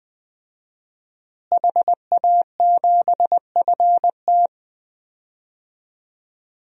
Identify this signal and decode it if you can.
Morse code, heard as a keyed tone, about 20 wpm, 700 Hz. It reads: HA7FT
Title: DXCC-100 Call Signs - 5 Characters - Repeated 3x - 20wpm